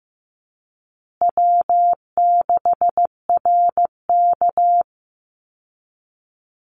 Morse code, heard as a keyed tone, about 15 wpm, 700 Hz. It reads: W6RK
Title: Easy Call Signs Repeated 3x - 15wpm